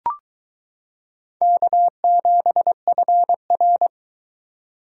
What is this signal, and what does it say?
Morse code, heard as a keyed tone, about 22 wpm, 700 Hz. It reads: K7FR